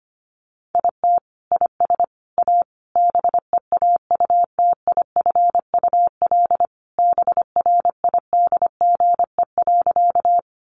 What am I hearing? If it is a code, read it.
Morse code, heard as a keyed tone, about 25 wpm, 700 Hz. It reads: IT IS A BEAUTIFUL BRIDGE.
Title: Sentences from Top 1200 Words 25wpm